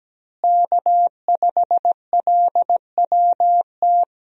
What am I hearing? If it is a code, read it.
Morse code, heard as a keyed tone, about 17 wpm, 700 Hz. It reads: K5LWT
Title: USA Call Signs - 5 Characters - Repeated 3x - 17wpm